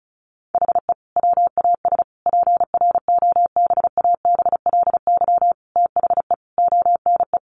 Morse code, heard as a keyed tone, about 34 wpm, 700 Hz. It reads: HE WAS PROBABLY THE ONE
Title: Sentences from Top 1300 Words 35wpm